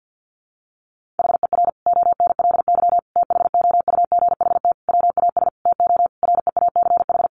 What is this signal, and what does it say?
Morse code, heard as a keyed tone, about 50 wpm, 700 Hz. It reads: HER ONLY THOUGHT WAS TO REACH
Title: Sentences from Top 400 Words 50wpm